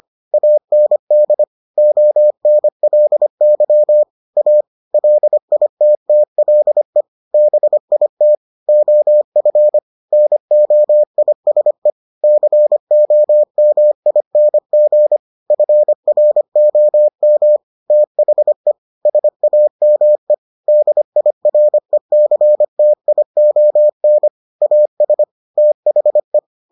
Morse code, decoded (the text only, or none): AND ONLY A LITTLE BIT OF NOISE COMING FROM THE SAME DIRECTION AS THE